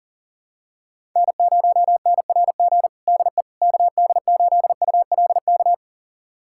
Morse code, heard as a keyed tone, about 37 wpm, 700 Hz. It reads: N0NRG DE KD8ULK
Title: USA Callsign DE Callsign 40wpm